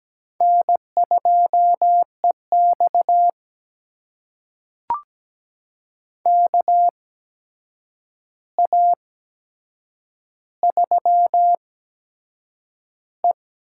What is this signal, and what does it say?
Morse code, heard as a keyed tone, about 17 wpm, spaced as for 7 wpm, 700 Hz. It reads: N2EX KA3E
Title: USA Call Signs - 4 Characters - 8x Character Spacing - 17wpm